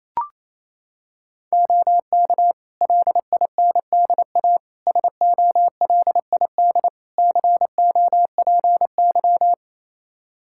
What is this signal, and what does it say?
Morse code, heard as a keyed tone, about 28 wpm, 700 Hz. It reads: OK LINDA SOLID COPY
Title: Practice QSO - Exchange 3 - More Info - 28wpm